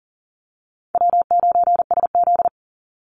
Morse code, heard as a keyed tone, about 40 wpm, 700 Hz. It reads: W9SZ
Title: Easy Call Signs - 40wpm